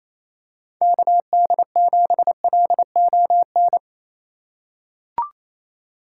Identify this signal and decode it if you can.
Morse code, heard as a keyed tone, about 28 wpm, 700 Hz. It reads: KD7LON
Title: USA Call Signs - 6 Characters - 2x Character Spacing - 28wpm